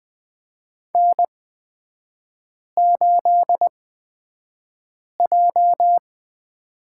Morse code, heard as a keyed tone, about 20 wpm, spaced as for 5 wpm, 700 Hz. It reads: N8J
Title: 255 Lesson 41e - Period - Three Characters 20wpm